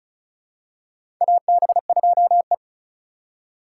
Morse code, heard as a keyed tone, about 36 wpm, 700 Hz. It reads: AB2E
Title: USA Call Signs - 4 Characters - 2x Character Spacing - 35wpm